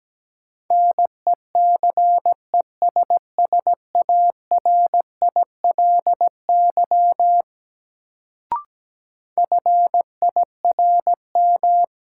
Morse code, heard as a keyed tone, about 17 wpm, 700 Hz. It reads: NECESSARILY FIRM
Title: Sentences from Top 1400 Words - New Word Drill Alt - 17wpm